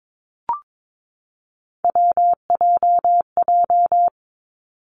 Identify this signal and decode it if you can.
Morse code, heard as a keyed tone, about 22 wpm, 700 Hz. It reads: WJJ
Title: Three Challenging Letter-Number Combinations 22wpm